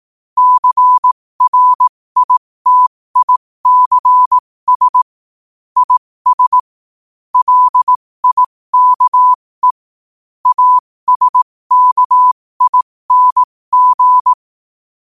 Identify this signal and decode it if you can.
Morse code, heard as a keyed tone, about 18 wpm, spaced as for 15 wpm, 1000 Hz. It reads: CRITICS IS LIKE ASKING
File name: qotdfiles / 15wpm